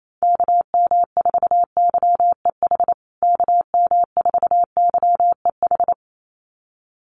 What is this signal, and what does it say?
Morse code, heard as a keyed tone, about 28 wpm, 700 Hz. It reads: KM4YEH KM4YEH
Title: USA Call Signs Repeated 2x 28wpm